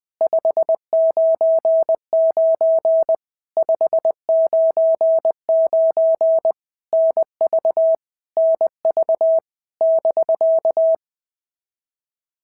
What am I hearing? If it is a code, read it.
Morse code, heard as a keyed tone, about 19 wpm, 650 Hz. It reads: 599 599 NV NV <BK>
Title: POTA QSO Practice - 20wpm